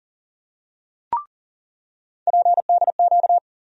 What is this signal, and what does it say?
Morse code, heard as a keyed tone, about 40 wpm, 700 Hz. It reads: PDQ
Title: USA Call Sign Postfixes - 3 Letters 40wpm